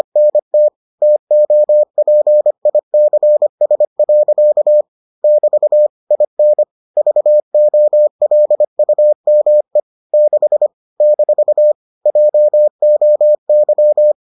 Morse code, heard as a keyed tone, about 25 wpm, 600 Hz. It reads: NT TOPICS. <BT> IN VOLUME 6 - JOY